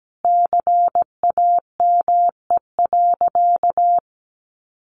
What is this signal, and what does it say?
Morse code, heard as a keyed tone, about 17 wpm, 700 Hz. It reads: CAME.
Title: Sentences from Top 1500 Words 17wpm